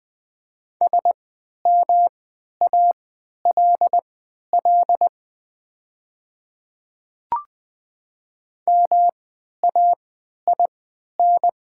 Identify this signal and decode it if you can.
Morse code, heard as a keyed tone, about 20 wpm, spaced as for 11 wpm, 700 Hz. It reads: SMALL MAIN
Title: 3x Character Spacing - Top 500 Words - 20wpm